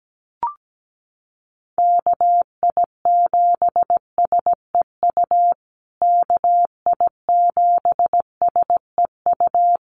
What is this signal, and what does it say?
Morse code, heard as a keyed tone, about 17 wpm, 700 Hz. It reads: KI7SEU KI7SEU
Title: USA Call Signs Repeated 3x 17wpm